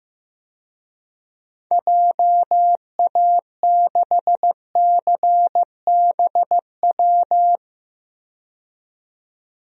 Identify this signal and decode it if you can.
Morse code, heard as a keyed tone, about 15 wpm, 700 Hz. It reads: JA6CBW
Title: DXCC-100 Call Signs - 6 Characters - 15wpm